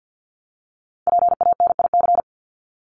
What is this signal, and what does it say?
Morse code, heard as a keyed tone, about 50 wpm, 700 Hz. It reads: PANIC
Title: Top 1751-2000 Words 50wpm